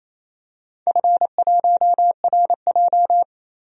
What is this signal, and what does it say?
Morse code, heard as a keyed tone, about 29 wpm, 690 Hz. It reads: F1RJ
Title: DXCC-100 Call Signs - 4 Characters - 28wpm